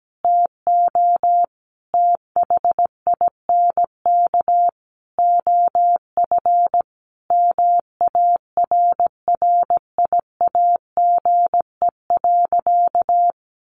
Morse code, heard as a keyed tone, about 17 wpm, 700 Hz. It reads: TO THINK OF MARRIAGE.
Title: Sentences from Top 800 Words 17wpm